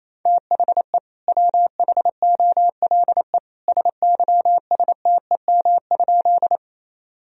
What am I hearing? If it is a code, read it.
Morse code, heard as a keyed tone, about 27 wpm, 700 Hz. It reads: THE WHOLE SYSTEM?
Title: Sentences from Top 1000 Words - Review Part 3 - 28wpm